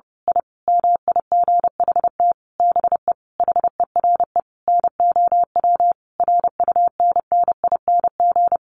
Morse code, heard as a keyed tone, about 30 wpm, 700 Hz. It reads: I MIGHT BE HERE NOW RUNNING
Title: A Sound Of Thunder 30wpm